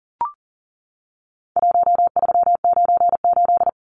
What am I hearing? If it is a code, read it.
Morse code, heard as a keyed tone, about 40 wpm, 700 Hz. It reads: 1398
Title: Numbers - Contest Style - 4-Digit 40wpm